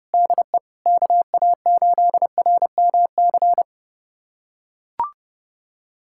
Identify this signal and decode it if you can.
Morse code, heard as a keyed tone, about 30 wpm, 700 Hz. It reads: DE KA8RMC
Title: USA Callsign DE Callsign - 6 Characters - 30wpm